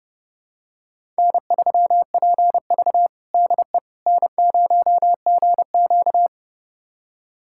Morse code, heard as a keyed tone, about 30 wpm, 700 Hz. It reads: N3PV DE N0GQ
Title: USA Callsign DE Callsign - 4 Characters - 30wpm